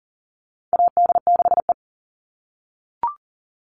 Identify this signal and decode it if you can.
Morse code, heard as a keyed tone, about 40 wpm, 700 Hz. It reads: AD6E
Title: Call Signs - 40wpm